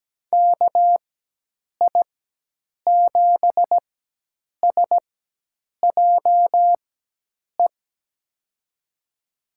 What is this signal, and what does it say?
Morse code, heard as a keyed tone, about 17 wpm, spaced as for 8 wpm, 700 Hz. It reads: KI7SJE
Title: USA Call Signs - 6 Characters - 4x Character Spacing - 17wpm